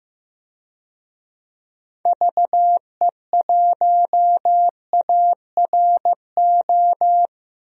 Morse code, heard as a keyed tone, about 15 wpm, 700 Hz. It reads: VE1ARO
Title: DXCC-100 Call Signs - 6 Characters - 15wpm